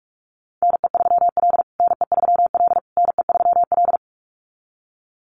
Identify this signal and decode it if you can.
Morse code, heard as a keyed tone, about 44 wpm, 700 Hz. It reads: NE3L NE3L NE3L
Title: USA Call Signs Repeated 3x 45wpm